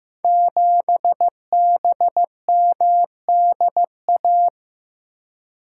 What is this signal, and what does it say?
Morse code, heard as a keyed tone, about 15 wpm, 700 Hz. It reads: 7BMDA
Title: Code Groups - 5 Letter:Numbers - 15wpm